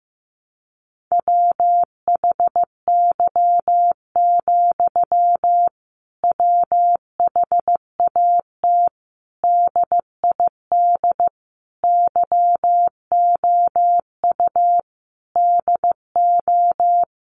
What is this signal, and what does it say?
Morse code, heard as a keyed tone, about 15 wpm, 700 Hz. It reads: WHY, WHAT DID YOU DO
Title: Sentences from Top 1500 Words 15wpm